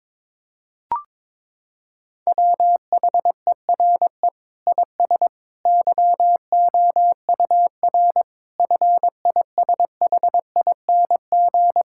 Morse code, heard as a keyed tone, about 21 wpm, 700 Hz. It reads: WHERE IS YOUR FISHING
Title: Sentences from Top 1600 Words 22wpm